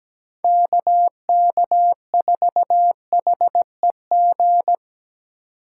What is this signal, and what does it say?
Morse code, heard as a keyed tone, about 17 wpm, 700 Hz. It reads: KK4HEG
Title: K1USN SST QSO Practice - 17wpm